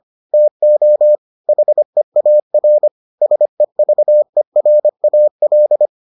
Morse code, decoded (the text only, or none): TO HEAR SEVERAL